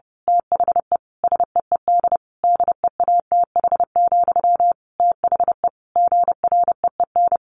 Morse code, heard as a keyed tone, about 30 wpm, 700 Hz. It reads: THE SEED DEATH, THE GREEN
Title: A Sound Of Thunder 30wpm